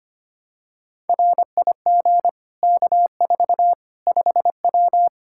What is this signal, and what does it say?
Morse code, heard as a keyed tone, about 25 wpm, 700 Hz. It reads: RIG K4 5W
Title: Practice QSO - Exchange 2 - Info - 25wpm